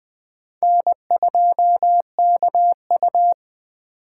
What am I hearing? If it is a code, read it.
Morse code, heard as a keyed tone, about 21 wpm, 700 Hz. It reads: N2KU
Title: Speed Racing - USA 1x2 Call Signs 17wpm